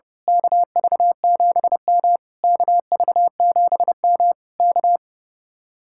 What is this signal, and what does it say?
Morse code, heard as a keyed tone, about 28 wpm, 700 Hz. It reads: KV7M KV7M K
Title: Speed-Racing Common QSO Phrases - 30wpm